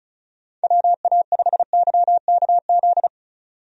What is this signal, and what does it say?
Morse code, heard as a keyed tone, about 34 wpm, 700 Hz. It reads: WA5YKZ
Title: USA Call Signs - 6 Characters - 35wpm